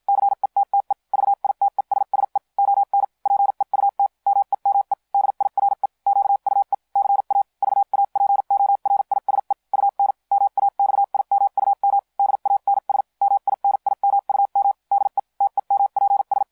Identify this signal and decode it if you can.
Morse code, heard as a keyed tone, about 53 wpm, spaced as for 44 wpm, 800 Hz. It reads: CETTE VITESSE ON PEUT MEME DIRE QUE CA VAPORISE UN MAXIMUM DANS MINIMUM DE TEMPS